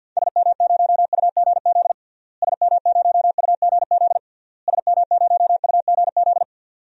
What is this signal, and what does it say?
Morse code, heard as a keyed tone, about 46 wpm, 700 Hz. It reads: SM0UGZ SM0UGZ SM0UGZ
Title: DXCC-100 Call Signs - 6 Characters - Repeated 3x - 50wpm